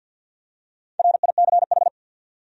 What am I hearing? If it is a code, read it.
Morse code, heard as a keyed tone, about 52 wpm, 700 Hz. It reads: RICH
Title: USA Common Names - Max 5 Letters - 50wpm